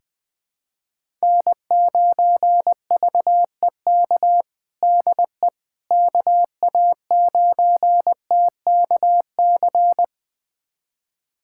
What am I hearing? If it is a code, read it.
Morse code, heard as a keyed tone, about 20 wpm, 700 Hz. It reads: N9VEK DE KA9TKC
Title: USA Callsign DE Callsign 20wpm